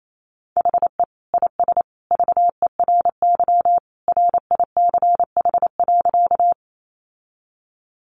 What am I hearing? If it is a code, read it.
Morse code, heard as a keyed tone, about 28 wpm, 700 Hz. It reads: HE IS VERY RICH.
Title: Sentences from Top 1100 Words 28wpm